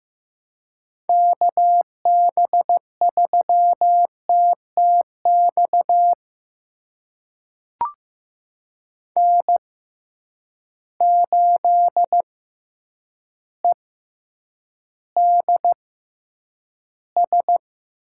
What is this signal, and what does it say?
Morse code, heard as a keyed tone, about 15 wpm, spaced as for 8 wpm, 700 Hz. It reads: KB3TTX N8EDS
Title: USA Call Signs - 6x Character Spacing - 15wpm